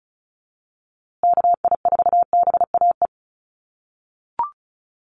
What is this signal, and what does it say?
Morse code, heard as a keyed tone, about 34 wpm, 700 Hz. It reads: KI4BAE
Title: USA Call Signs Repeated 2x 35wpm